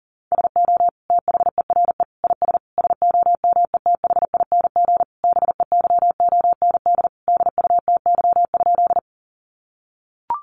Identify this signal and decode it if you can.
Morse code, heard as a keyed tone, about 40 wpm, 700 Hz. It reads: SO THERE IS SOMETHING BEYOND DUTY?
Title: Sentences from Top 1500 Words 40wpm